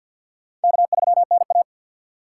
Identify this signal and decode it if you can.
Morse code, heard as a keyed tone, about 48 wpm, 700 Hz. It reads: K3NA
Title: USA Call Signs - 4 Characters - 6x Character Spacing - 50wpm